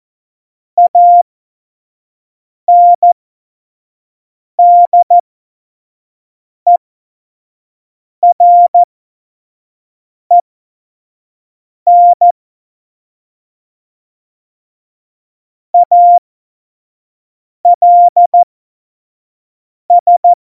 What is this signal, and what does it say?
Morse code, heard as a keyed tone, about 14 wpm, spaced as for 5 wpm, 700 Hz. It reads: ANDEREN ALS
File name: Morsekurs / Witze / Witz 11. Dezember 2018, 25 F70